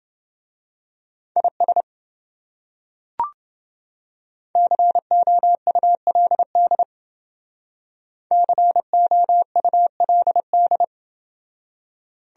Morse code, heard as a keyed tone, about 29 wpm, 700 Hz. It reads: IS COULD COULD